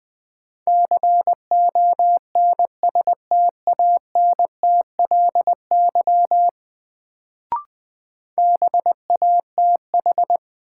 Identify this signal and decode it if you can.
Morse code, heard as a keyed tone, about 21 wpm, 700 Hz. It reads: CONSTANTLY BATH
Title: Sentences from Top 1600 Words - New Word Drill Alt - 20wpm